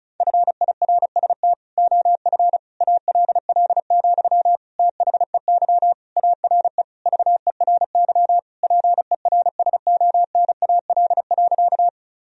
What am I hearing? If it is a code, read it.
Morse code, heard as a keyed tone, about 34 wpm, 700 Hz. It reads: FIRST OF ALL, THEY ARE VERY PERSONAL.
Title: Sentences from Top 1000 Words - Review Part 4 - 35wpm